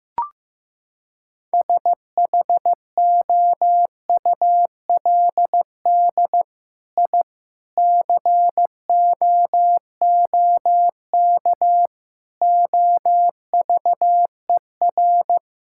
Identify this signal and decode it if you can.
Morse code, heard as a keyed tone, about 15 wpm, 700 Hz. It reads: SHOULD I COOK OVER
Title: Sentences from Top 1000 Words - Review Part 6 - 15wpm